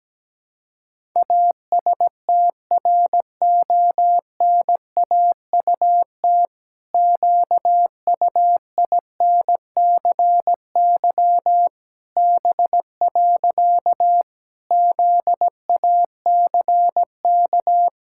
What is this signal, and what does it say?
Morse code, heard as a keyed tone, about 17 wpm, 700 Hz. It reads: ASTRONAUT QUINCY B. ZACK
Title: Pangram Challenge v2 - 17wpm